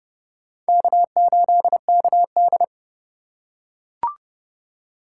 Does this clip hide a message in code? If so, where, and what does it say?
Yes: Morse code from 0.7 to 2.6 s, heard as a keyed tone, about 27 wpm, 700 Hz. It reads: K8KD